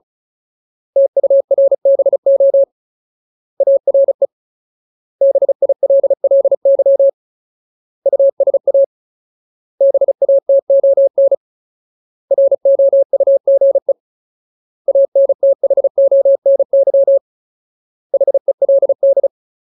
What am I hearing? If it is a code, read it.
Morse code, heard as a keyed tone, about 36 wpm, 550 Hz. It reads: TURBO ARE BILLY USA BATON ROUGE ANTHONY HELD
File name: Ultimate Word List / 35WPM Ultimate